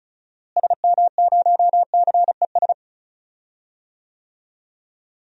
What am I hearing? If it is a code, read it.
Morse code, heard as a keyed tone, about 34 wpm, 700 Hz. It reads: SM0CES
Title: DXCC-100 Call Signs - 6 Characters - Repeated 2x - 35wpm